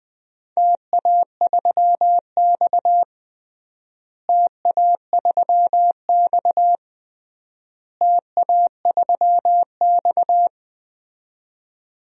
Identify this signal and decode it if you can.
Morse code, heard as a keyed tone, about 20 wpm, 700 Hz. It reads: TA3X TA3X TA3X
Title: DXCC-100 Call Signs - 4 Characters - Repeated 3x - 20wpm